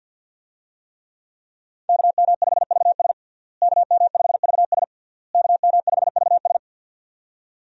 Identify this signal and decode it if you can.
Morse code, heard as a keyed tone, about 47 wpm, 700 Hz. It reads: KM5VS KM5VS KM5VS
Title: USA Call Signs - 5 Characters - Repeated 3x - 50wpm